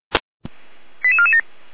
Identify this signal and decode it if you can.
Five-tone selective call (ZVEI), heard as quick runs of five tones, tones of about 70 ms. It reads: 88447